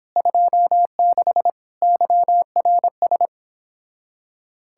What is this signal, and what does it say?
Morse code, heard as a keyed tone, about 25 wpm, 700 Hz. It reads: 26 YRS
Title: Speed-Racing Common QSO Phrases - 22wpm